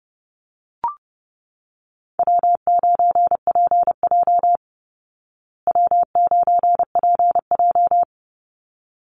Morse code, heard as a keyed tone, about 30 wpm, 700 Hz. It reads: W9PJ W9PJ
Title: Call Signs Repeated 3x - 30wpm